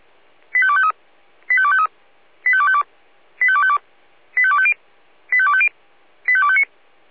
Five-tone selective call (ZVEI), heard as quick runs of five tones, tones of about 70 ms. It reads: 85253 85253 85252 85252 85250 85250 85259
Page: /496/SDR/ZVEI/